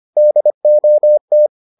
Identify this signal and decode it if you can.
Morse code, heard as a keyed tone, about 25 wpm, 600 Hz. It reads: DOT